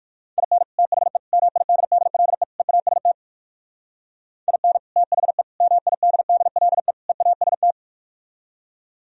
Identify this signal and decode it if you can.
Morse code, heard as a keyed tone, about 58 wpm, 700 Hz. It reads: IN THE MIDDLE EAST IN THE MIDDLE EAST